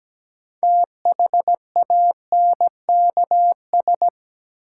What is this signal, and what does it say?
Morse code, heard as a keyed tone, about 17 wpm, 700 Hz. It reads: THANKS
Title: Sets of 5 Words - New Word Drill - 17wpm